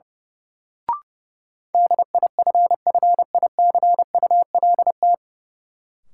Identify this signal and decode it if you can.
Morse code, heard as a keyed tone, about 30 wpm, 700 Hz. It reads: DIFFICULT
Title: Sets of 4 Words - New Word Drill - 30wpm